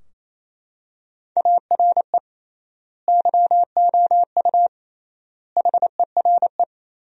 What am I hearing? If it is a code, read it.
Morse code, heard as a keyed tone, about 27 wpm, 700 Hz. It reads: ARE YOU HERE
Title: 3x Word-Spacing - Sentences from Top 200 Words - 28wpm